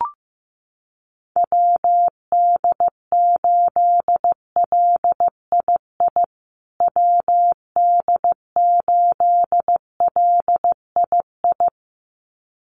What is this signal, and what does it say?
Morse code, heard as a keyed tone, about 15 wpm, 700 Hz. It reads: WD8LII WD8LII
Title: USA Call Signs Repeated 2x 15wpm